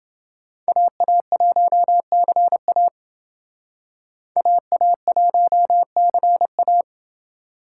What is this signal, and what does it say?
Morse code, heard as a keyed tone, about 28 wpm, 700 Hz. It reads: AA1CA AA1CA